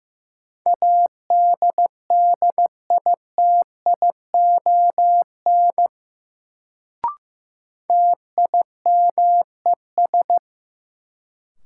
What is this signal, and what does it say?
Morse code, heard as a keyed tone, about 15 wpm, 700 Hz. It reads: ADDITION TIMES
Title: Sentences from Top 1000 Words - Review Part 6 - New Word Drill - 15wpm